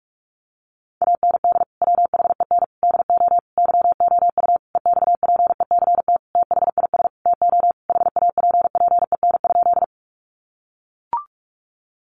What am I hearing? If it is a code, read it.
Morse code, heard as a keyed tone, about 44 wpm, 700 Hz. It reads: AND WHEN DO YOU EXPECT THIS TO HAPPEN?
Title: Sentences from Top 1100 Words 45wpm